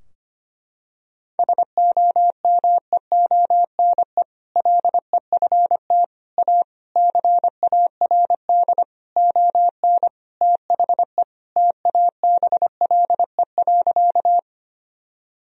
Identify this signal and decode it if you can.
Morse code, heard as a keyed tone, about 25 wpm, 700 Hz. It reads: SOMEONE LEFT A CARD ON THE TABLE.
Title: Sentences from Top 1000 Words - Review Part 4 - 25wpm